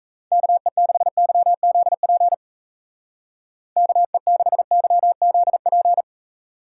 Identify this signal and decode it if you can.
Morse code, heard as a keyed tone, about 34 wpm, 700 Hz. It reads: KE6YZP KE6YZP
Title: Speed Racing - USA 2x3 Call Signs 28wpm